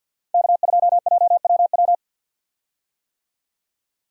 Morse code, heard as a keyed tone, about 50 wpm, 700 Hz. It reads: K2JWW
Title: USA Call Signs - 5 Characters - 50wpm